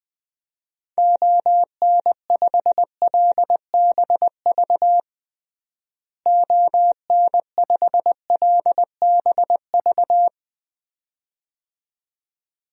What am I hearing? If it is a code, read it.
Morse code, heard as a keyed tone, about 20 wpm, 700 Hz. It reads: ON5LBV ON5LBV
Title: DXCC-100 Call Signs - 6 Characters - Repeated 2x - 20wpm